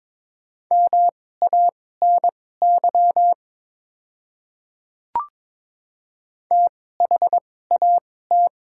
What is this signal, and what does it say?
Morse code, heard as a keyed tone, about 22 wpm, spaced as for 16 wpm, 690 Hz. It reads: MANY THAT
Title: 2x Character Spacing - Top 100 Words - 22wpm